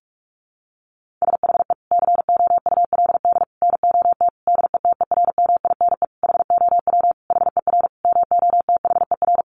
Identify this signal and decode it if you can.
Morse code, heard as a keyed tone, about 44 wpm, 700 Hz. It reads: SHE COULD NOT DETERMINE HOW HER MOTHER
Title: Sentences from Top 700 Words 45wpm